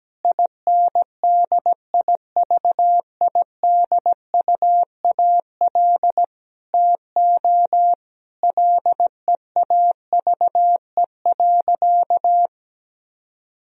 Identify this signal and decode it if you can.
Morse code, heard as a keyed tone, about 17 wpm, 700 Hz. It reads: INDIVIDUAL TO LEAVE.
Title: Sentences from Top 1000 Words - Review Part 4 - 17wpm